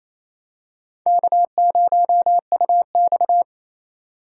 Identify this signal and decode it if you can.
Morse code, heard as a keyed tone, about 28 wpm, 700 Hz. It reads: K0UX
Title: USA Call Signs - 3x Character Spacing - 28wpm